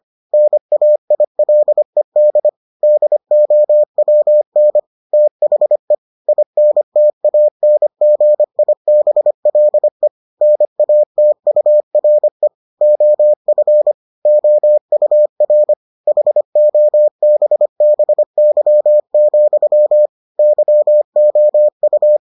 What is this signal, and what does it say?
Morse code, heard as a keyed tone, about 25 wpm, 600 Hz. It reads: NAILED DOWN THE INTANGIBLE NATURE OF OUR HOBBY, YOU